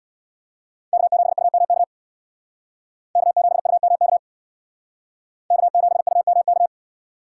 Morse code, heard as a keyed tone, about 62 wpm, 700 Hz. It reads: K7UMK K7UMK K7UMK
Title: Speed Racing - USA 1x3 Call Signs 50wpm